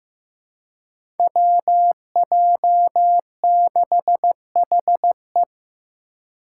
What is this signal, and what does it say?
Morse code, heard as a keyed tone, about 15 wpm, 700 Hz. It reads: WJ6HE
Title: Five Challenging Letter-Number Combinations 15wpm